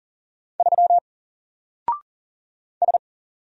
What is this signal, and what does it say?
Morse code, heard as a keyed tone, about 40 wpm, 700 Hz. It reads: 3 S
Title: Single Letter-Number Alt 40wpm